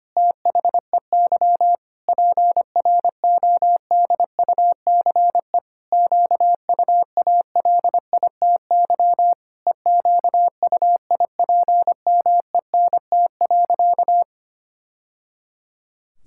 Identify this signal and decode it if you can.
Morse code, heard as a keyed tone, about 25 wpm, 700 Hz. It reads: THEY PRODUCE QUALITY EQUIPMENT.
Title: Sentences from Top 1000 Words - Review Part 6 - 25wpm